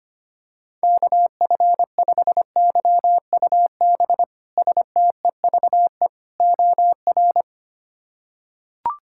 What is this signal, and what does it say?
Morse code, heard as a keyed tone, about 25 wpm, 700 Hz. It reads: KF5YUB STEVE OR
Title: K1USN SST QSO Practice - 25wpm